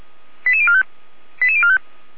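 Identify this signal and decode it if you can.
Five-tone selective call (ZVEI), heard as quick runs of five tones, tones of about 70 ms. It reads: 88045 88045
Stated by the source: Five-tone ZVEI selective call received on SDR, fire brigade Feuerwehr Orlamünde.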